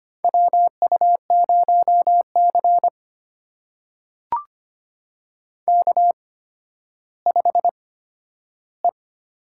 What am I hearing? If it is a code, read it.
Morse code, heard as a keyed tone, about 25 wpm, spaced as for 12 wpm, 700 Hz. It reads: WU0C K5E
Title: USA Call Signs - 4 Characters - 8x Character Spacing - 25wpm